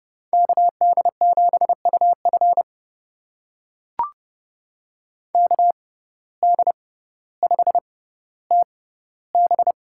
Morse code, heard as a keyed tone, about 32 wpm, spaced as for 16 wpm, 690 Hz. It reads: KD7UF KD5TB
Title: USA Call Signs - 5 Characters - 6x Character Spacing - 30wpm